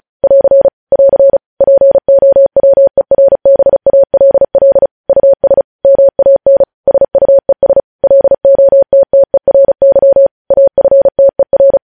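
Morse code, heard as a keyed tone, about 36 wpm, 550 Hz. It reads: <AR> <AR> POWERBALL US MAN SUES LOTTERY AFTER
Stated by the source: Morse news headlines sent at 35 words per minute.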